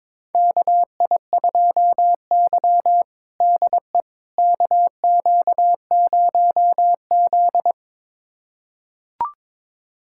Morse code, heard as a keyed tone, about 22 wpm, 700 Hz. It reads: KI2Y DE KQ0Z
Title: USA Callsign DE Callsign - 4 Characters - 22wpm